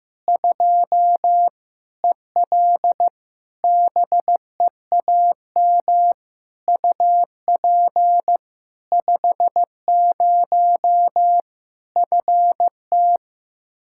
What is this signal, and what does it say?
Morse code, heard as a keyed tone, about 15 wpm, 700 Hz. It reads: 2 EL BEAM UP 50 FT
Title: Practice QSO - Exchange 2 - Info - 15wpm